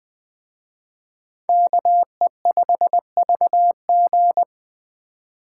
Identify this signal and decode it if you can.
Morse code, heard as a keyed tone, about 23 wpm, 700 Hz. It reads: KE5VG
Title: Speed Racing - USA 2x2 Call Signs 20wpm